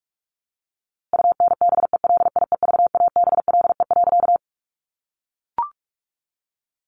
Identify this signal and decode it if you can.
Morse code, heard as a keyed tone, about 44 wpm, 700 Hz. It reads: UNBELIEVABLE.
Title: A Sound Of Thunder 45wpm